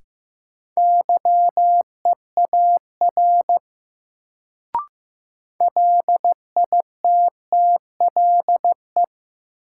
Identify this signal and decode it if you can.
Morse code, heard as a keyed tone, about 15 wpm, 700 Hz. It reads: YEAR LITTLE
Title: Sentences from Top 200 Words - New Word Drill - 15wpm